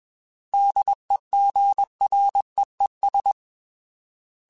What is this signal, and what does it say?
Morse code, heard as a keyed tone, about 21 wpm, 800 Hz. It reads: DEGREES
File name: Weather Sets / Weather Sets-mp3s / Weather 4 14 @20